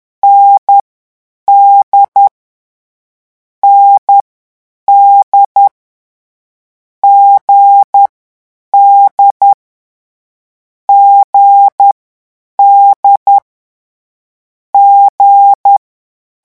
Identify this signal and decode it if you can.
Morse code, heard as a keyed tone, about 10 wpm, 800 Hz. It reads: N D N D G D G D G
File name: morse mp3 / lecon4-vitesse 10